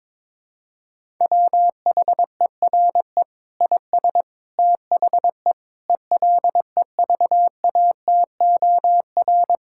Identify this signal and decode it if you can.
Morse code, heard as a keyed tone, about 22 wpm, 700 Hz. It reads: WHERE IS THE ELEVATOR
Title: Sentences from Top 1600 Words 22wpm